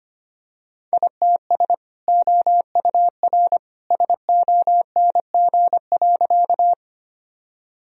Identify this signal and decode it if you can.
Morse code, heard as a keyed tone, about 25 wpm, 700 Hz. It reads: ITS OUR SONG.
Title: Sentences from Top 1000 Words - Review Part 2 - 25wpm